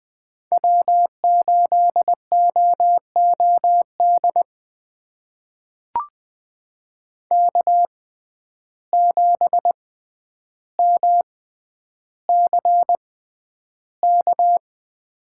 Morse code, heard as a keyed tone, about 20 wpm, spaced as for 10 wpm, 700 Hz. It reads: W8OOD K7MCK
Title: USA Call Signs - 5 Characters - 6x Character Spacing - 20wpm